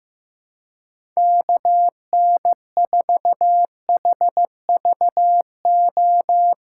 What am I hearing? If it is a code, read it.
Morse code, heard as a keyed tone, about 15 wpm, 700 Hz. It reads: KN4HVO
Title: USA Call Signs - 6 Characters - 4x Character Spacing - 15wpm